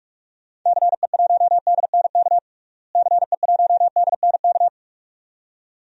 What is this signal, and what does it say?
Morse code, heard as a keyed tone, about 42 wpm, 700 Hz. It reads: CE1DNK CE1DNK
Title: DXCC-100 Call Signs - 6 Characters - Repeated 2x - 45wpm